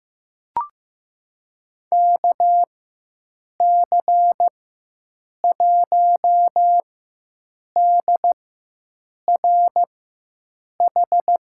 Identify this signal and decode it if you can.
Morse code, heard as a keyed tone, about 15 wpm, spaced as for 7 wpm, 700 Hz. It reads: KC1DRH
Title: USA Call Signs - 6 Characters - 4x Character Spacing - 15wpm